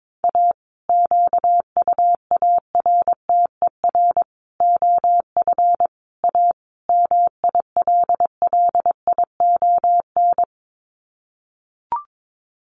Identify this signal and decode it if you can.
Morse code, heard as a keyed tone, about 22 wpm, 700 Hz. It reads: A QUARTER OF A MILLION
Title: Sets of 5 Words Encore2 22wpm